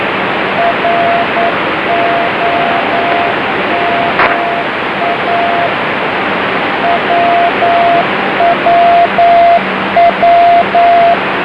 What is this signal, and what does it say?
Morse code, heard as a keyed tone, about 9 wpm, 690 Hz, with radio noise. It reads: ROMA WWW
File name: BEACONS / i0jx